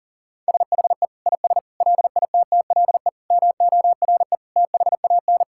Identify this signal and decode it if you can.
Morse code, heard as a keyed tone, about 40 wpm, 700 Hz. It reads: SHE IS LITTLE MORE THAN
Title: Sentences from Top 1600 Words 40wpm